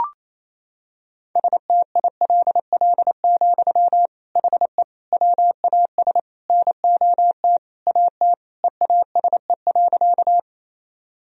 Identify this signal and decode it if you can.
Morse code, heard as a keyed tone, about 28 wpm, 700 Hz. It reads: STILL, HE WAS NOT AT EASE.
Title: Sentences from Top 1000 Words - Review Part 1 - 28wpm